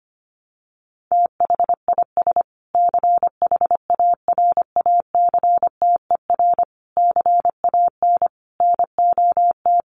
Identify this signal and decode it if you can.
Morse code, heard as a keyed tone, about 25 wpm, 700 Hz. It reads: THIS CHARACTER CAN NOT
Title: Sets of 5 Words Encore2 25wpm